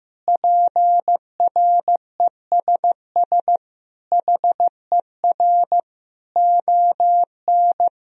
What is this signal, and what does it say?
Morse code, heard as a keyed tone, about 15 wpm, 700 Hz. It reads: PRESS HER ON
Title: Sentences from Top 1000 Words - Review Part 1 - 15wpm